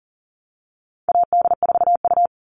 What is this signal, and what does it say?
Morse code, heard as a keyed tone, about 40 wpm, 700 Hz. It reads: AD4U
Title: All Cut Numbers - 4-Digit 40wpm